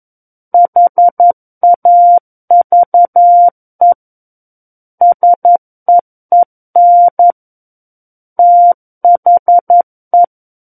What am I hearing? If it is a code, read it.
Morse code, heard as a keyed tone, about 11 wpm, 700 Hz. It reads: HAVE SEEN THE
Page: SKCC - War of the Worlds - 02-Chapter 2 - 11 WPM